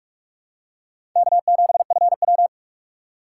Morse code, heard as a keyed tone, about 46 wpm, 700 Hz. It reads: K7FW
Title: USA Call Signs - 4 Characters - 3x Character Spacing - 45wpm